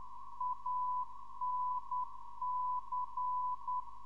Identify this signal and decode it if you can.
Morse code, heard as a keyed tone, about 10 wpm, 1000 Hz. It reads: ANC
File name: ndb zvok / anc